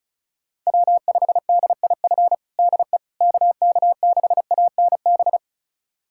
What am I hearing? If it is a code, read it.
Morse code, heard as a keyed tone, about 34 wpm, 700 Hz. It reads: W5DIF DE KK6ANB
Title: USA Callsign DE Callsign 35wpm